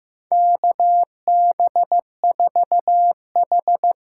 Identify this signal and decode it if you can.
Morse code, heard as a keyed tone, about 15 wpm, 700 Hz. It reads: KB4H
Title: USA Call Signs - 6 Characters - 8x Character Spacing - 15wpm